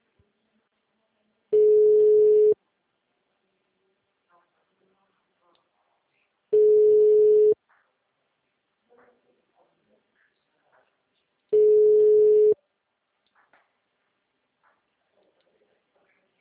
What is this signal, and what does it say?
Ringback tone (Europe, 425 Hz).
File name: out- -2030- - - .18533